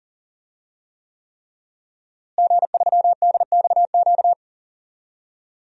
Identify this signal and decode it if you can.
Morse code, heard as a keyed tone, about 37 wpm, 700 Hz. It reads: G3DXQ
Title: DXCC-100 Call Signs - 5 Characters - Repeated 3x - 40wpm